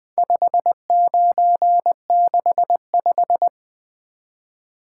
Morse code, heard as a keyed tone, about 20 wpm, 700 Hz. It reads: 5965
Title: Speed-Racing - Four Letter-Number Combinations - 20wpm